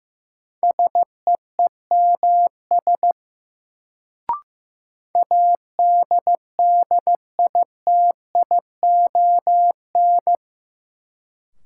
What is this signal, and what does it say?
Morse code, heard as a keyed tone, about 15 wpm, 700 Hz. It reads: SEEMS ADDITION
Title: Sets of 3 Words - New Word Drill - 15wpm